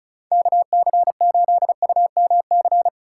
Morse code, heard as a keyed tone, about 36 wpm, 700 Hz. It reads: KC8UMC
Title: USA Call Signs - 6 Characters - 4x Character Spacing - 35wpm